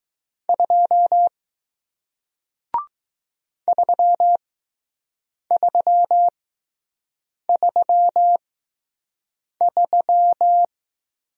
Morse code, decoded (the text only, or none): 2 3333